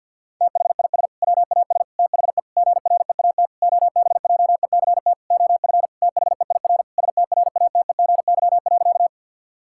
Morse code, heard as a keyed tone, about 52 wpm, 700 Hz. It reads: THIS WAS THE GREAT OBJECT OF THEIR STRATEGY.
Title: Sentences from Top 1000 Words - Review Part 5 - 50wpm